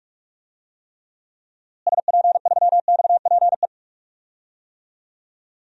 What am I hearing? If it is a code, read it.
Morse code, heard as a keyed tone, about 46 wpm, 700 Hz. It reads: SP3XPE
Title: DXCC-100 Call Signs - 6 Characters - Repeated 2x - 45wpm